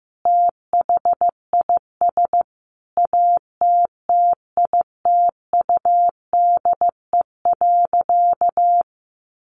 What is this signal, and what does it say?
Morse code, heard as a keyed tone, about 14 wpm, 700 Hz. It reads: THIS ATTITUDE.
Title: Sentences from Top 1300 Words 15wpm